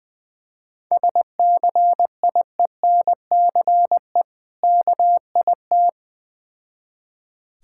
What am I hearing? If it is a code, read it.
Morse code, heard as a keyed tone, about 20 wpm, 700 Hz. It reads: SCIENCE KIT
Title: Difficult Words with Hints - 20wpm